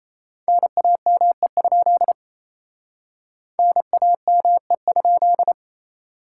Morse code, heard as a keyed tone, about 36 wpm, 700 Hz. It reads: NAME? NAME?